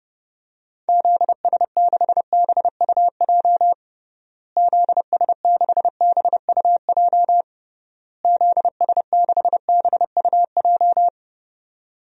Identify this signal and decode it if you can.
Morse code, heard as a keyed tone, about 30 wpm, 700 Hz. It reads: ZS6BUJ ZS6BUJ ZS6BUJ
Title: DXCC-100 Call Signs - 6 Characters - Repeated 3x - 30wpm